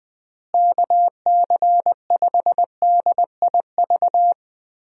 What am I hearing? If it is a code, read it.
Morse code, heard as a keyed tone, about 20 wpm, 700 Hz. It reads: KC5DIV
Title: USA Call Signs - 6 Characters - Repeated 2x - 20wpm